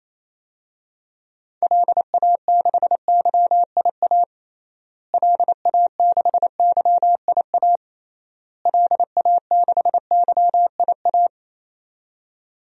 Morse code, heard as a keyed tone, about 28 wpm, 700 Hz. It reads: LA6YIA LA6YIA LA6YIA
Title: DXCC-100 Call Signs - 6 Characters - Repeated 3x - 28wpm